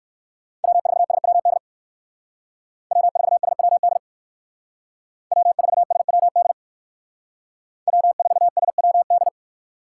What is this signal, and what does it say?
Morse code, heard as a keyed tone, about 59 wpm, 700 Hz. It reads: W4SWD W4SWD W4SWD W4SWD